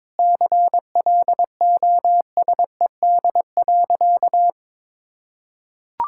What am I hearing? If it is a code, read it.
Morse code, heard as a keyed tone, about 22 wpm, 700 Hz. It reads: CLOSED.
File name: Sentences with 7-Word Phrases - 22wpm